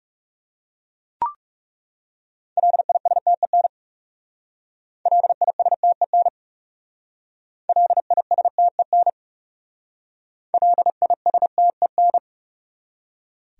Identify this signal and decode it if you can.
Morse code, heard as a keyed tone, about 38 wpm, 700 Hz. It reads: LISTEN LISTEN LISTEN LISTEN